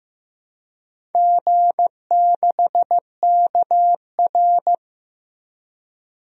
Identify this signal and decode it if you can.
Morse code, heard as a keyed tone, about 15 wpm, 700 Hz. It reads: G6KR
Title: DXCC-100 Call Signs - 4 Characters - Repeated 2x - 15wpm